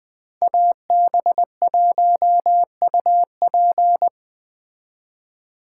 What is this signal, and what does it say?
Morse code, heard as a keyed tone, about 21 wpm, 700 Hz. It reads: AB1UP